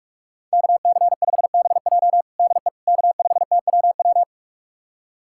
Morse code, heard as a keyed tone, about 46 wpm, 700 Hz. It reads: KC5BJ DE K5TWW
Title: USA Callsign DE Callsign - 5 Characters - 45wpm